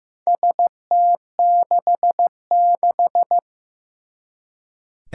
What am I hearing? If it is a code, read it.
Morse code, heard as a keyed tone, about 15 wpm, 700 Hz. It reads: ST66
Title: Four Challenging Letter-Number Combinations 15wpm